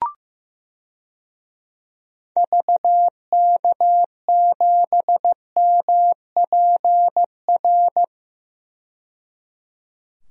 Morse code, heard as a keyed tone, about 15 wpm, 700 Hz. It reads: VK7MPR
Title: DXCC-100 Call Signs - 6 Characters - 15wpm